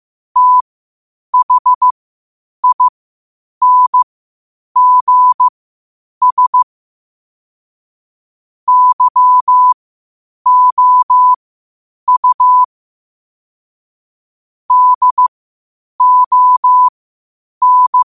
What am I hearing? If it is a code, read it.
Morse code, heard as a keyed tone, about 15 wpm, spaced as for 8 wpm, 1000 Hz. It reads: THINGS YOU DON
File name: qotdfiles / 8wpm